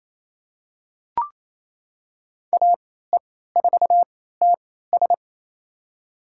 Morse code, heard as a keyed tone, about 28 wpm, spaced as for 16 wpm, 700 Hz. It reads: AE4TS
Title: USA Call Signs - 3x Character Spacing - 28wpm